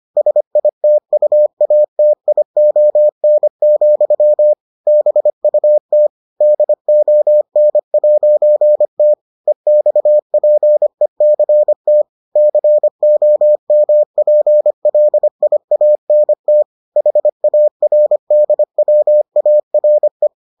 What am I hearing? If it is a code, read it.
Morse code, heard as a keyed tone, about 25 wpm, 600 Hz. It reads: SITUATION, BUT DON'T EXPECT COMPLIANT HARDWARE